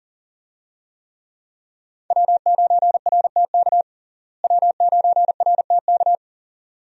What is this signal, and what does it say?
Morse code, heard as a keyed tone, about 40 wpm, 700 Hz. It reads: W9RTK W9RTK
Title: USA Call Signs - 5 Characters - Repeated 2x - 40wpm